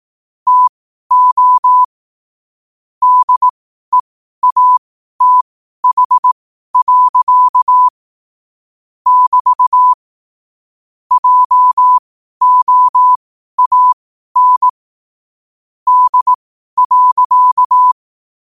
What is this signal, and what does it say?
Morse code, heard as a keyed tone, about 18 wpm, spaced as for 12 wpm, 1000 Hz. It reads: TO DEATH. <BT> JOAN D.
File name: qotdfiles / 12wpm